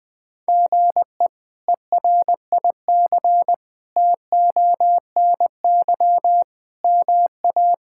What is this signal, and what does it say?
Morse code, heard as a keyed tone, about 20 wpm, 700 Hz. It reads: GE ERIC TONY MA
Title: K1USN SST QSO Practice - 20wpm